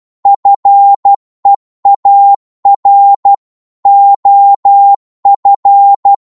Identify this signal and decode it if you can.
Morse code, heard as a keyed tone, about 12 wpm, 800 Hz. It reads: FEAR OF
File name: cwmp3 / psg-1-12wpm